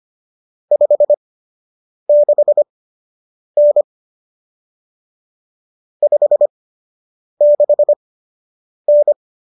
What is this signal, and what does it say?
Morse code, heard as a keyed tone, about 25 wpm, spaced as for 8 wpm, 590 Hz. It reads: 56N 56N
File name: qso006 08